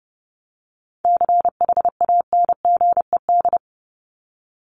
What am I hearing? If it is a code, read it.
Morse code, heard as a keyed tone, about 30 wpm, 700 Hz. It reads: CHANGED
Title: ED Suffix - Top 100 Words with Base Words 30wpm